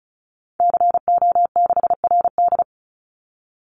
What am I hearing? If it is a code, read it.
Morse code, heard as a keyed tone, about 34 wpm, 700 Hz. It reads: CO6RD
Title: Call Signs Repeated 3x - 35wpm